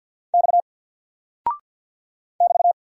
Morse code, heard as a keyed tone, about 48 wpm, 700 Hz. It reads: X <BT>
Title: Single Character (Letter:Number:Punctuation:Prosign) 50wpm